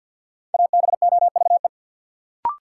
Morse code, heard as a keyed tone, about 53 wpm, 700 Hz. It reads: ABOVE
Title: Sentences from Top 1200 Words - New Word Drill Alt - 50wpm